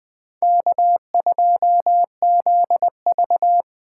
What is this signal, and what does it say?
Morse code, heard as a keyed tone, about 20 wpm, 700 Hz. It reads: K2ZV